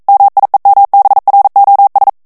Morse code, heard as a keyed tone, about 42 wpm, 800 Hz. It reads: MIEMBROS
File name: morse mp3 / ESPAGNOL TEXTE EN CLAIR 1 VITESSE 40 MOTS MINUTES